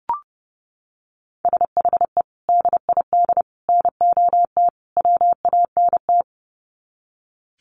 Morse code, heard as a keyed tone, about 30 wpm, 700 Hz. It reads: SHE DID NOT WANT
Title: Sets of 4 Words 30wpm